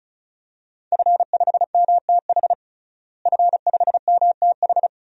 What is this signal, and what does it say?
Morse code, heard as a keyed tone, about 36 wpm, 700 Hz. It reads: F5MTH F5MTH
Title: DXCC-100 Call Signs - 5 Characters - Repeated 2x - 35wpm